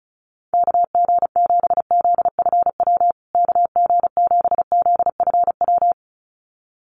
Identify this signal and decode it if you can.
Morse code, heard as a keyed tone, about 34 wpm, 700 Hz. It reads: KG7ZFW KG7ZFW
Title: USA Call Signs Repeated 2x 35wpm